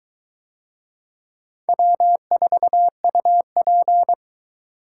Morse code, heard as a keyed tone, about 22 wpm, 700 Hz. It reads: W4UP